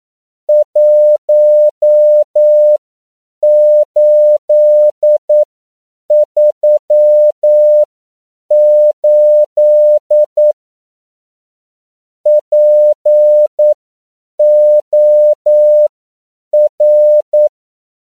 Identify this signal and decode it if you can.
Morse code, heard as a keyed tone, about 9 wpm, spaced as for 7 wpm, 600 Hz. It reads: 1838 POR
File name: CW / 10PPM